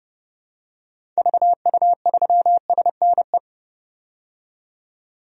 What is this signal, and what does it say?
Morse code, heard as a keyed tone, about 30 wpm, 700 Hz. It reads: VU3SNE
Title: DXCC-100 Call Signs - 6 Characters - 30wpm